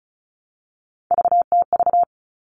Morse code, heard as a keyed tone, about 34 wpm, 700 Hz. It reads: VTV
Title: All Cut Numbers - 3-Digit 35wpm